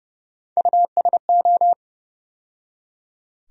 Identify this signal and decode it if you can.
Morse code, heard as a keyed tone, about 30 wpm, 700 Hz. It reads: USO
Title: USA Call Sign Postfixes - 3 Letters 30wpm